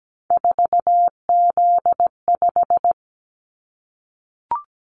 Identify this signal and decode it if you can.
Morse code, heard as a keyed tone, about 17 wpm, 700 Hz. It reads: 4Z5
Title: Callsign Prefixes for Top 50 DXCC - 17wpm